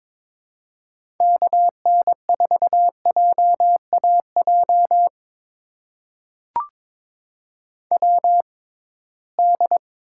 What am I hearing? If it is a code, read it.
Morse code, heard as a keyed tone, about 22 wpm, spaced as for 16 wpm, 700 Hz. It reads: KN4JAJ WD
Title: USA Call Signs - 6 Characters - 6x Character Spacing - 22wpm